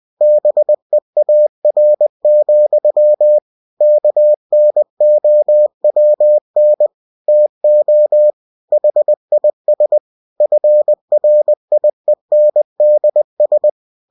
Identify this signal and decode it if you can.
Morse code, heard as a keyed tone, about 20 wpm, 600 Hz. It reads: BEAR, KNOWN TO HIS FRIENDS